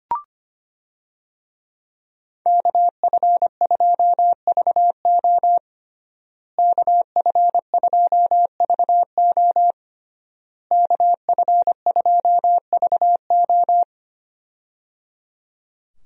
Morse code, heard as a keyed tone, about 25 wpm, 700 Hz. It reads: KF2VO KF2VO KF2VO
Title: USA Call Signs - 5 Characters - Repeated 3x - 25wpm